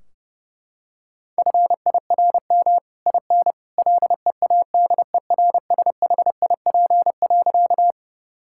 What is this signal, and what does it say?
Morse code, heard as a keyed tone, about 30 wpm, 700 Hz. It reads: FIRM IN LEADERSHIP.
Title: Sentences from Top 1600 Words 30wpm